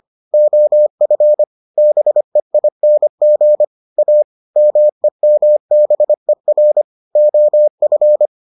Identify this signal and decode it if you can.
Morse code, heard as a keyed tone, about 25 wpm, 600 Hz. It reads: OF BEING A MEMBER OF